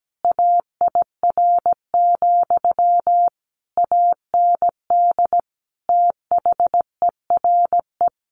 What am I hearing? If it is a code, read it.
Morse code, heard as a keyed tone, about 17 wpm, 700 Hz. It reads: AIR, AND THERE
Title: A Sound Of Thunder 17wpm